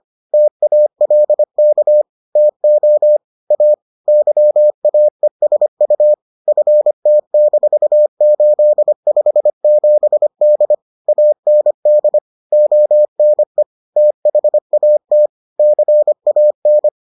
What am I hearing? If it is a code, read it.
Morse code, heard as a keyed tone, about 25 wpm, 600 Hz. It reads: TALK TO A YAESU FT-857D AND ONE THAT CAN